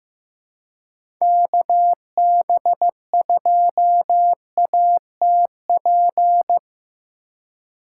Morse code, heard as a keyed tone, about 16 wpm, 700 Hz. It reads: KB2ATP
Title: USA Call Signs - 2x Character Spacing - 15wpm